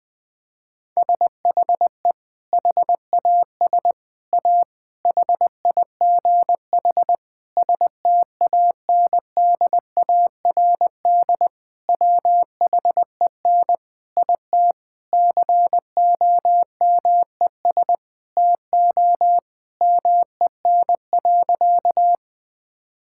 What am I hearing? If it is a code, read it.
Morse code, heard as a keyed tone, about 21 wpm, 700 Hz. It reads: SHE HAS A HIGH STANDARD WHEN IT COMES TO MEN.
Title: Sentences from Top 1000 Words - Review Part 6 - 20wpm